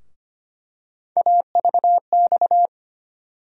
Morse code, heard as a keyed tone, about 25 wpm, 700 Hz. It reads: AVX
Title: USA Call Sign Postfixes - 3 Letters 25wpm